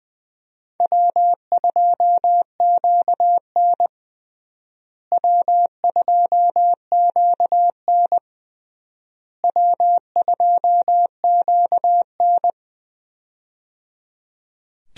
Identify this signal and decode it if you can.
Morse code, heard as a keyed tone, about 20 wpm, 700 Hz. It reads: W2QN W2QN W2QN
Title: USA Call Signs - 4 Characters - Repeated 3x - 20wpm